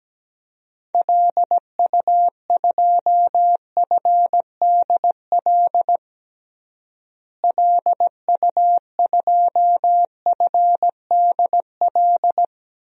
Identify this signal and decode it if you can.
Morse code, heard as a keyed tone, about 17 wpm, 700 Hz. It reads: LU2FDL LU2FDL
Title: DXCC-100 Call Signs - 6 Characters - Repeated 3x - 17wpm